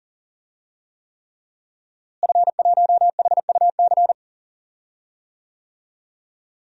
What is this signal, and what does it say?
Morse code, heard as a keyed tone, about 40 wpm, 700 Hz. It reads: F1HUC
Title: DXCC-100 Call Signs - 5 Characters - Repeated 3x - 40wpm